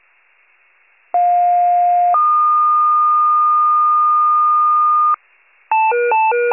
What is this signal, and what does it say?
Two-tone page, heard as a one-second tone, then a three-second tone, about 710 then 1220 Hz. It is heard from 1.1 to 5.2 s.
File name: wcalert